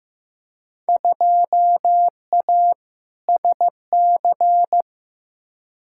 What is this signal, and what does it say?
Morse code, heard as a keyed tone, about 15 wpm, 700 Hz. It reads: 2A SC
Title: Field Day Exchanges - 15wpm